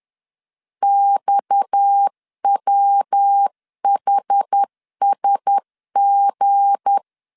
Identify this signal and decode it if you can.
Morse code, heard as a keyed tone, about 10 wpm, 800 Hz. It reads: XWHSG
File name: LEZIONI / 7 GF9Z / 7 GF 10wpm 2514